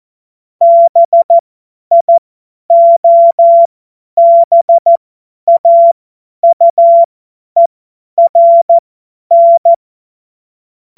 Morse code, heard as a keyed tone, about 14 wpm, spaced as for 10 wpm, 680 Hz. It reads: BIOBAUERN
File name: Morsekurs / Witze / Witz 27. April 2021, 50 F70